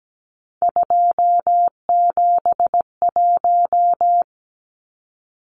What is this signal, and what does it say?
Morse code, heard as a keyed tone, about 17 wpm, 700 Hz. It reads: 271
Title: Numbers - 3-Digit 17wpm